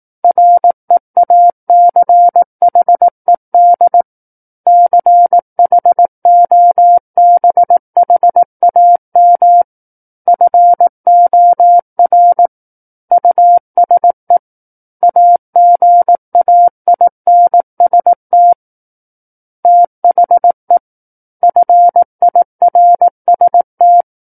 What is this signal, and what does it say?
Morse code, heard as a keyed tone, about 18 wpm, 700 Hz. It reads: REACHED CHOBHAM FOR USE AGAINST THE FIRST
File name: 18 WPM